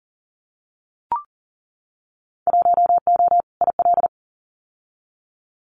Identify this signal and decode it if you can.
Morse code, heard as a keyed tone, about 40 wpm, 700 Hz. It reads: 1O IL
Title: Winter Field Day Exchanges 40wpm